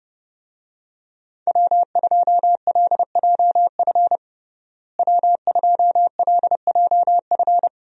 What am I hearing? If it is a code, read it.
Morse code, heard as a keyed tone, about 32 wpm, 700 Hz. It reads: W2LJF W2LJF
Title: USA Call Signs - 5 Characters - Repeated 2x - 30wpm